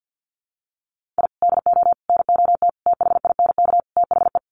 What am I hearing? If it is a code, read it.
Morse code, heard as a keyed tone, about 50 wpm, 700 Hz. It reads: I DO NOT THINK THE
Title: Sets of 5 Words 50wpm